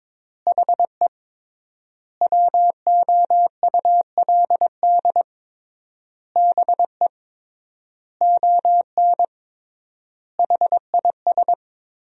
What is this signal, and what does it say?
Morse code, heard as a keyed tone, about 22 wpm, 700 Hz. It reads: HE WOULD BE ON HIS
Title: 3x Word-Spacing - Sentences from Top 300 Words - 22wpm